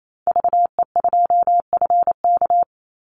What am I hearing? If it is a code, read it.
Morse code, heard as a keyed tone, about 28 wpm, 700 Hz. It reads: VE2FK
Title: Call Signs - 28wpm